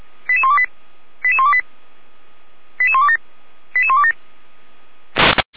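Five-tone selective call (ZVEI), heard as quick runs of five tones, tones of about 70 ms. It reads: 88128 88128 88127 88127
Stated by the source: ZVEI five-tone selective call 88235 received by SDR, from the feuerwehr-orlamuende.org (Orlamünde fire brigade) audio server.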